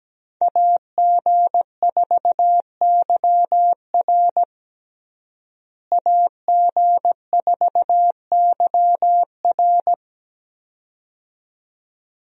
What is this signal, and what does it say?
Morse code, heard as a keyed tone, about 17 wpm, 700 Hz. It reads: AG4YR AG4YR
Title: USA Call Signs - 5 Characters - Repeated 2x - 17wpm